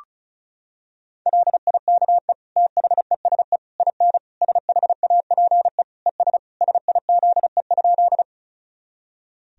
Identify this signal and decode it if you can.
Morse code, heard as a keyed tone, about 36 wpm, 700 Hz. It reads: LIKE THESE IN SHAPE ES SIZE?
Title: Sentences from Top 500 Words and Abbreviations 35wpm